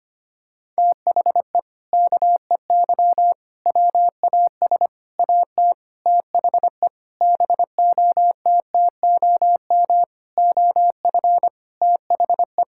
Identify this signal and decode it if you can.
Morse code, heard as a keyed tone, about 25 wpm, 700 Hz. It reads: THE KEY WAS AT THE BOTTOM OF THE
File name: practice / 25wpm / Sentences with 4-Word Phrases - 25wpm